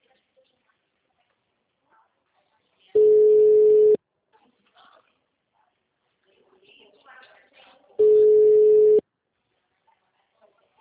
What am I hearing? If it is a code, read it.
Ringback tone (Europe, 425 Hz), with speech.
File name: out- -2010- - - .37148